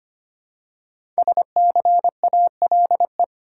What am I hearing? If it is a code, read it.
Morse code, heard as a keyed tone, about 25 wpm, 700 Hz. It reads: SCALE
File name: 5 Letter Words - First 3 Letters Given - 25wpm